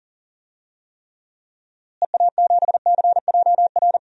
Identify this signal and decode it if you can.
Morse code, heard as a keyed tone, about 38 wpm, 700 Hz. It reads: EA7CJR
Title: DXCC-100 Call Signs - 6 Characters - Repeated 3x - 40wpm